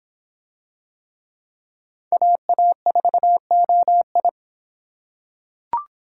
Morse code, heard as a keyed tone, about 25 wpm, 700 Hz. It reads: AA4OI